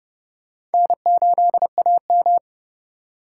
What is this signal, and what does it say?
Morse code, heard as a keyed tone, about 28 wpm, 700 Hz. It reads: N8AM